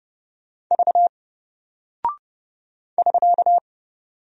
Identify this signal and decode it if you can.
Morse code, heard as a keyed tone, about 30 wpm, 700 Hz. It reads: V <SK>
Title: Single Character (Letter:Number:Punctuation:Prosign) Alt 30wpm